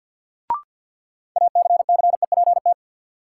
Morse code, heard as a keyed tone, about 50 wpm, 700 Hz. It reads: ACCEPT
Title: Sentences from Top 1300 Words - New Word Drill - 50wpm